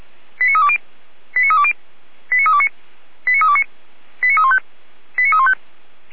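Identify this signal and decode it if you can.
Five-tone selective call (ZVEI), heard as quick runs of five tones, tones of about 80 ms. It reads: 89320 89320 89329 89329 89315 89315
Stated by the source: ZVEI five-tone selective call received by SDR, logged by Feuerwehr Orlamünde fire brigade.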